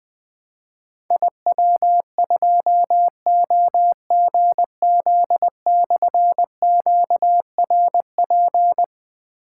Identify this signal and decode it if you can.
Morse code, heard as a keyed tone, about 20 wpm, 700 Hz. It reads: IW2OGZ/QRP
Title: Difficult Call Signs 20wpm